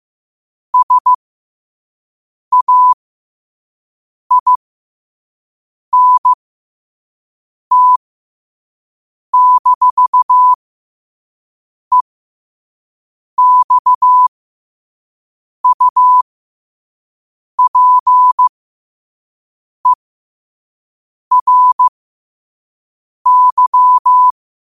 Morse code, heard as a keyed tone, about 15 wpm, spaced as for 5 wpm, 1000 Hz. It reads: SAINT-EXUPERY